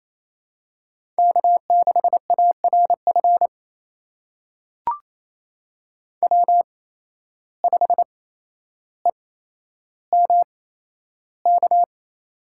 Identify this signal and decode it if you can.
Morse code, heard as a keyed tone, about 28 wpm, spaced as for 12 wpm, 700 Hz. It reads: K6ARF W5EMK
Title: USA Call Signs - 5 Characters - 8x Character Spacing - 28wpm